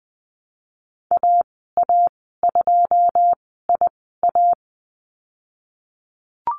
Morse code, heard as a keyed tone, about 20 wpm, spaced as for 15 wpm, 700 Hz. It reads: AA2IA
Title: 088 Lesson 15g - Number 5 - Call Signs 20wpm